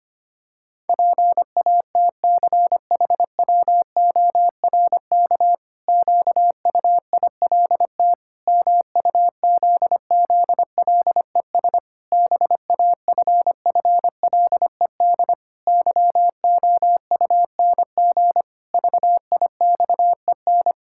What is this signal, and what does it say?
Morse code, heard as a keyed tone, about 25 wpm, 700 Hz. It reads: PATCHWORK QUILT MUZZLES BAFFLED YOUNG VIXEN
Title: Pangram Challenge v2 - 25wpm